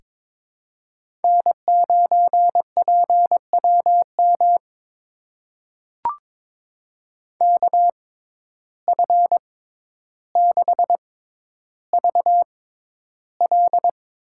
Morse code, heard as a keyed tone, about 22 wpm, spaced as for 11 wpm, 700 Hz. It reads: N9PWM KF6VL
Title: USA Call Signs - 6x Character Spacing - 22wpm